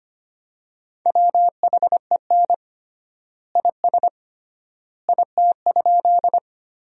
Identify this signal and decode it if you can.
Morse code, heard as a keyed tone, about 25 wpm, 700 Hz. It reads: WHEN IS IT?
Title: 3x Word-Spacing - Sentences from Top 100 Words - 25wpm